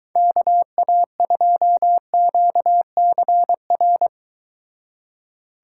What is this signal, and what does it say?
Morse code, heard as a keyed tone, about 23 wpm, 700 Hz. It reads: KA2QCR